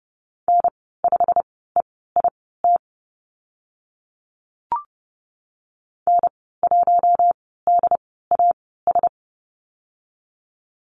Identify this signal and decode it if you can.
Morse code, heard as a keyed tone, about 30 wpm, spaced as for 17 wpm, 700 Hz. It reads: N5EIT N1DAS
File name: Lesson 15g - Number 5 - Call Signs 30wpm